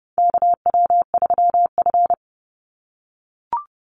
Morse code, heard as a keyed tone, about 30 wpm, 700 Hz. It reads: KW3F
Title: Call Signs - 30wpm